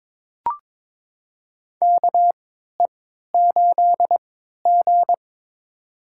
Morse code, heard as a keyed tone, about 22 wpm, spaced as for 13 wpm, 700 Hz. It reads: KE8G
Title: USA Call Signs - 4 Characters - 3x Character Spacing - 22wpm